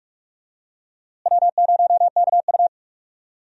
Morse code, heard as a keyed tone, about 46 wpm, 700 Hz. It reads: W0KU
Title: USA Call Signs - 4 Characters - Repeated 2x - 45wpm